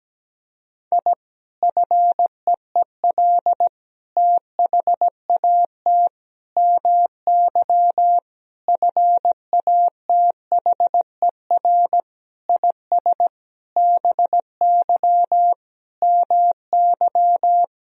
Morse code, heard as a keyed tone, about 17 wpm, 700 Hz. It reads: I FEEL THAT MY FATHER IS BY MY
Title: Sentences from Top 1000 Words - Review Part 1 - 17wpm